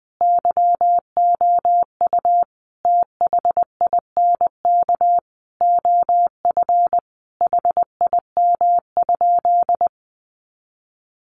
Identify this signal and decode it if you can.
Morse code, heard as a keyed tone, about 20 wpm, 700 Hz. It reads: YOU THINK OF HIM?
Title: Sentences from Top 200 Words 20wpm